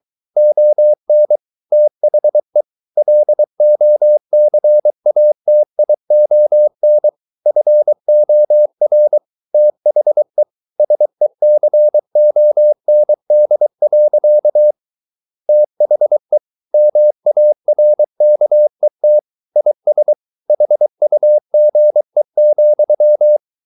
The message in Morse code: ON THE LOCATION FOR THE SECOND. THE MARKET IS HUGE,

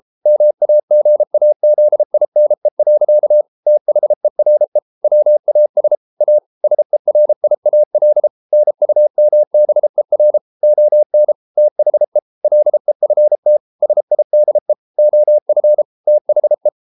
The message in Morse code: MAGAZINE. THERE WAS A SERIAL NUMBER ON THE LEFT SIDE OF THE

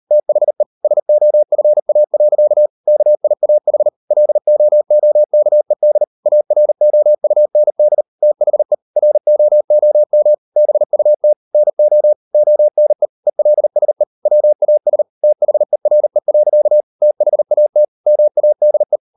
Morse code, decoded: THE SOFA. KIAH LOOKED AROUND THE ROOM BUT NO ONE ELSE WAS THERE. THAT MADE

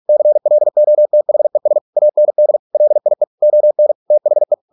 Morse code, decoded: CLOTHES AND LIE ON THE

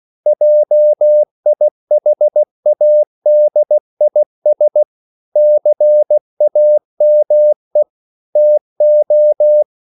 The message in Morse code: JIHADIS CAME TO

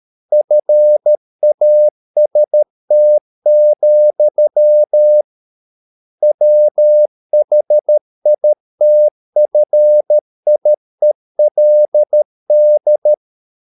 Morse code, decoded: FAST, WHITFIELD

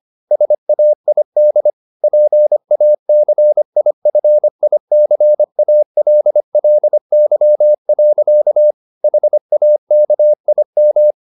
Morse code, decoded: SAID PACIFICALLY. HAKIM

